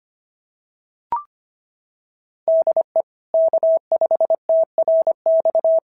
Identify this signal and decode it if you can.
Morse code, heard as a keyed tone, about 25 wpm, 660 Hz. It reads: DE K5TRX
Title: SOTA QSO Practice 25wpm